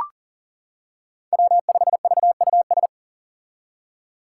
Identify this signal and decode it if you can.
Morse code, heard as a keyed tone, about 40 wpm, 700 Hz. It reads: W5VUS
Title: Five Challenging Letter-Number Combinations 40wpm